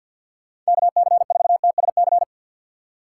Morse code, heard as a keyed tone, about 52 wpm, 700 Hz. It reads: KC4TSC
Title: USA Call Signs - 6 Characters - 4x Character Spacing - 50wpm